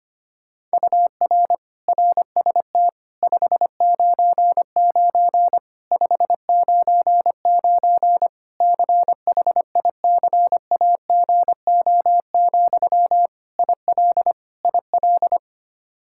Morse code, heard as a keyed tone, about 25 wpm, 700 Hz. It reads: UR RST 599 599 CHICAGO, IL IL
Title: Practice QSO - Exchange 1 - Beginning - 25wpm